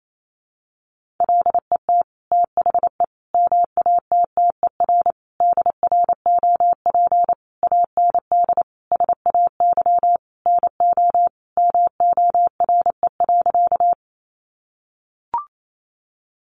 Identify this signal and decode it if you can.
Morse code, heard as a keyed tone, about 28 wpm, 700 Hz. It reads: LET THE MATTER DROP AND SAY NO MORE.
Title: Sentences from Top 1400 Words 28wpm